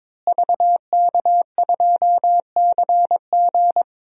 Morse code, heard as a keyed tone, about 22 wpm, 700 Hz. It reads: VK2CG
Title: DXCC-100 Call Signs - 5 Characters - Repeated 2x - 22wpm